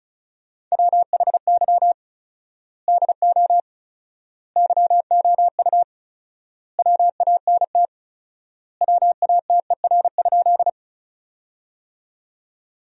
Morse code, heard as a keyed tone, about 36 wpm, 700 Hz. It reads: WHY DO YOU WANT WATER?
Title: 4x Word-Spacing - Sentences from Top 200 Words - 35wpm